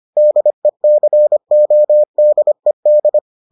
Morse code, heard as a keyed tone, about 25 wpm, 600 Hz. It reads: DECODED